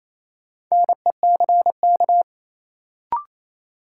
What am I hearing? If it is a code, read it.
Morse code, heard as a keyed tone, about 28 wpm, 700 Hz. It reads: NECK
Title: Sets of 5 Words Encore - New Word Drill - 28wpm